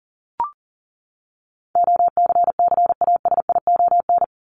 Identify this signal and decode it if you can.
Morse code, heard as a keyed tone, about 40 wpm, 700 Hz. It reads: OCCASION
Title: Top 1501-1750 Words 40wpm